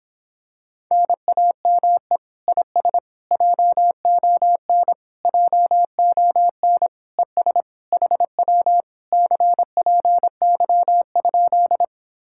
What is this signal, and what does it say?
Morse code, heard as a keyed tone, about 26 wpm, 700 Hz. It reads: NAME IS JON JON ES HW CPY?